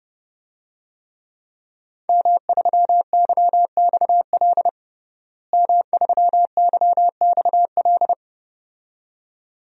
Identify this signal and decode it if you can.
Morse code, heard as a keyed tone, about 32 wpm, 700 Hz. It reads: M3YXL M3YXL
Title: DXCC-100 Call Signs - 5 Characters - Repeated 2x - 30wpm